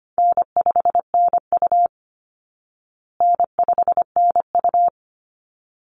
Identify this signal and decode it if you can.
Morse code, heard as a keyed tone, about 25 wpm, 700 Hz. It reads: N5NU N5NU
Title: Easy Call Signs Repeated 2x - 25wpm